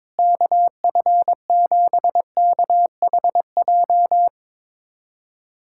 Morse code, heard as a keyed tone, about 22 wpm, 700 Hz. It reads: KF7KHJ
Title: USA Callsign DE Callsign - 6 Characters - 22wpm